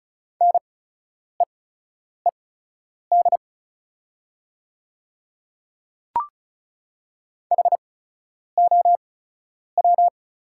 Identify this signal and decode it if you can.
Morse code, heard as a keyed tone, about 36 wpm, spaced as for 10 wpm, 690 Hz. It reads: NEED HOW
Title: 8x Character Spacing - Top 500 Words - 35wpm